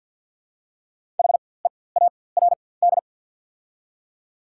Morse code, heard as a keyed tone, about 48 wpm, spaced as for 23 wpm, 690 Hz. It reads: HEARD
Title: 4x Character Spacing - Top 500 Words - 50wpm